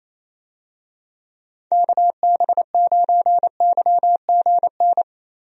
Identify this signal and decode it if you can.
Morse code, heard as a keyed tone, about 27 wpm, 700 Hz. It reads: KB9YGN
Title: USA Call Signs - 6 Characters - Repeated 3x - 28wpm